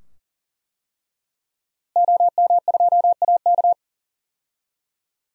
Morse code, heard as a keyed tone, about 37 wpm, 700 Hz. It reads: OM2AK
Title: DXCC-100 Call Signs - 5 Characters - Repeated 2x - 40wpm